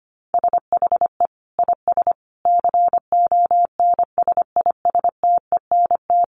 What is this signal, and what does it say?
Morse code, heard as a keyed tone, about 25 wpm, 700 Hz. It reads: SHE IS CONSISTENT
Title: Sentences from Top 1500 Words 25wpm